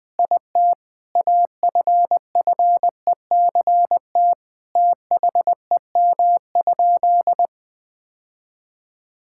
Morse code, heard as a keyed tone, about 21 wpm, 700 Hz. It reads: IT AFFECT THEM?
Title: Sentences from Top 1000 Words - Review Part 6 - 20wpm